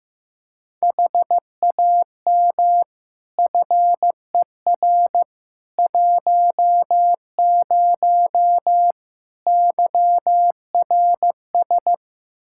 Morse code, heard as a keyed tone, about 15 wpm, 700 Hz. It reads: HAM FER 10 YRS
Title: Practice QSO - Exchange 3 - More Info - 15wpm